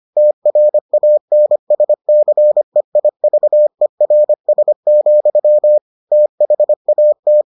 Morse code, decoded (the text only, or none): TRANSCEIVERS, THAT